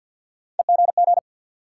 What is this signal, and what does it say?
Morse code, heard as a keyed tone, about 48 wpm, 700 Hz. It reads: EGG
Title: Three Letter Words - 50wpm